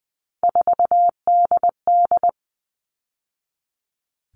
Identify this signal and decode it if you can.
Morse code, heard as a keyed tone, about 20 wpm, 700 Hz. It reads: 4DD
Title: All Cut Numbers - 3-Digit 20wpm